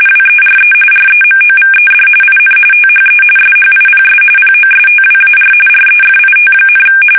RTTY, about 45.45 baud. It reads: WERE REMOVED IN THE SHIPS POSTWAR REFIT,
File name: fsk850 / USS-Massachusetts